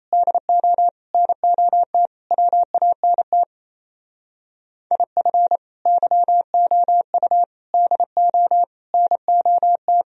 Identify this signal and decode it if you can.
Morse code, heard as a keyed tone, about 35 wpm, 700 Hz. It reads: DO NOT WANT IF YOU DO NOT
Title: Speed-Racing - Sets of 5 Words 28wpm